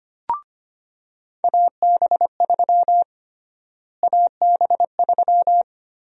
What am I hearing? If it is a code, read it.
Morse code, heard as a keyed tone, about 25 wpm, 700 Hz. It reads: AB3 AB3
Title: USA Call Sign Prefixes - Repeated 2x - 25wpm